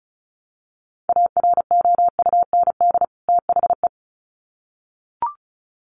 Morse code, heard as a keyed tone, about 34 wpm, 700 Hz. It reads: AROUND THE
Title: Sets of 2 Words 35wpm